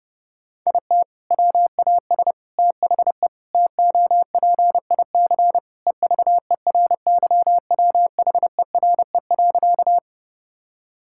Morse code, heard as a keyed tone, about 30 wpm, 700 Hz. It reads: IT WAS THE TOPIC EVERYWHERE.
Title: Sentences from Top 1600 Words 30wpm